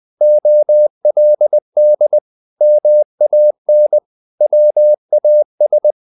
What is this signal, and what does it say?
Morse code, heard as a keyed tone, about 20 wpm, 600 Hz. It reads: OLD MAN WAS
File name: the old man and the sea / the old man and the sea part 1 20wpm